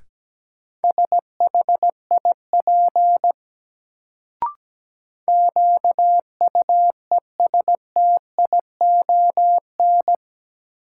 Morse code, heard as a keyed tone, about 17 wpm, 700 Hz. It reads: SHIP QUESTION
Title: Sentences from Top 400 Words - New Word Drill - 17wpm